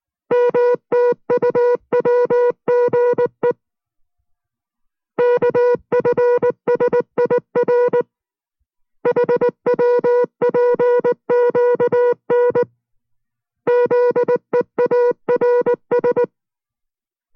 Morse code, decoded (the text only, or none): MTUWGE KFSIR HWPQN ZEARS